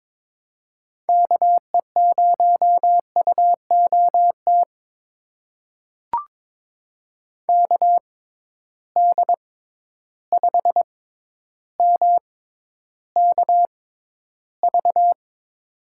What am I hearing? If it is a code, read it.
Morse code, heard as a keyed tone, about 22 wpm, spaced as for 11 wpm, 700 Hz. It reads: KE0UOT KD5MKV
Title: USA Call Signs - 6 Characters - 6x Character Spacing - 22wpm